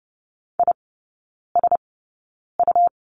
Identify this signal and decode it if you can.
Morse code, heard as a keyed tone, about 30 wpm, spaced as for 9 wpm, 700 Hz. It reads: ISU
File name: Lesson 17e - Letter u - Three Characters 30wpm